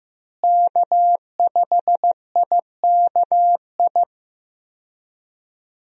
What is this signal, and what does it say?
Morse code, heard as a keyed tone, about 15 wpm, 700 Hz. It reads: K5IKI
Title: USA Call Signs - 5 Characters - Repeated 3x - 15wpm